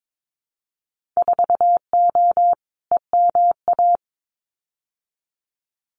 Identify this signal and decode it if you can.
Morse code, heard as a keyed tone, about 22 wpm, 700 Hz. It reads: 4O EMA
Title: Winter Field Day Exchanges 22wpm